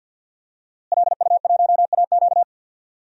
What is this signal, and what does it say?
Morse code, heard as a keyed tone, about 49 wpm, 700 Hz. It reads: LU1AQ
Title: DXCC-100 Call Signs - 5 Characters - Repeated 2x - 50wpm